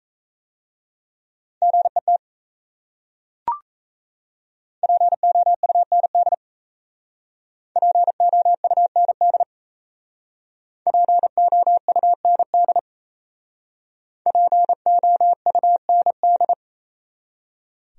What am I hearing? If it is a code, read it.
Morse code, heard as a keyed tone, about 36 wpm, 700 Hz. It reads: GET POUND POUND POUND POUND